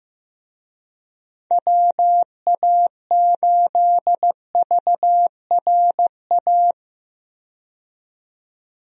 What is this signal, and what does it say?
Morse code, heard as a keyed tone, about 15 wpm, 700 Hz. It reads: WA8VRA
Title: USA Call Signs - 6 Characters - 15wpm